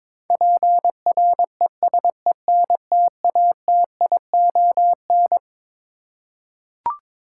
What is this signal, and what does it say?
Morse code, heard as a keyed tone, about 22 wpm, 700 Hz. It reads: PRESENTATION
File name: Twelve Letter Words - 22wpm